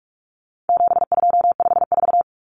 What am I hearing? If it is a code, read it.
Morse code, heard as a keyed tone, about 44 wpm, 700 Hz. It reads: 7254
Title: Numbers - 4-Digit 45wpm